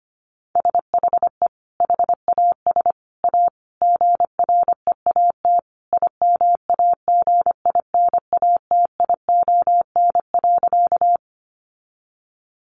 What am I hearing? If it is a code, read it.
Morse code, heard as a keyed tone, about 25 wpm, 700 Hz. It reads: SHE HAS A GREAT IMAGINATION.
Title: Sentences from Top 1300 Words 25wpm